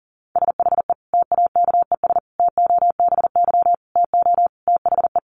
Morse code, heard as a keyed tone, about 40 wpm, 700 Hz. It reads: SHE TAKES TOBY TO THE
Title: Easy Phrases 40wpm